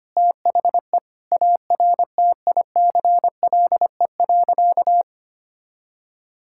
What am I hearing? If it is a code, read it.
Morse code, heard as a keyed tone, about 25 wpm, 700 Hz. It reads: THE ARTICLE.
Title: Sentences from Top 1000 Words - Review Part 5 - 25wpm